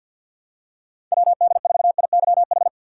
Morse code, heard as a keyed tone, about 52 wpm, 700 Hz. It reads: WD4IYH
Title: USA Call Signs - 6 Characters - 3x Character Spacing - 50wpm